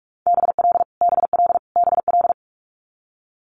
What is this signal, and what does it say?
Morse code, heard as a keyed tone, about 46 wpm, 700 Hz. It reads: BL BL BL
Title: Multi-letter Phonemes Repeated 3x 45wpm